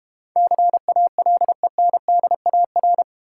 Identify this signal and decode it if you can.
Morse code, heard as a keyed tone, about 32 wpm, 700 Hz. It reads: CALENDAR